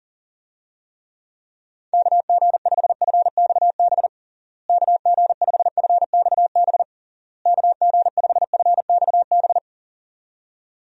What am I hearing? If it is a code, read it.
Morse code, heard as a keyed tone, about 40 wpm, 700 Hz. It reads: KG5FXB KG5FXB KG5FXB
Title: USA Call Signs - 6 Characters - Repeated 3x - 40wpm